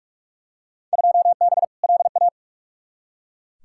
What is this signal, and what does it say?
Morse code, heard as a keyed tone, about 45 wpm, 700 Hz. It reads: 2B LA
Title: Field Day Exchanges Encore - 45wpm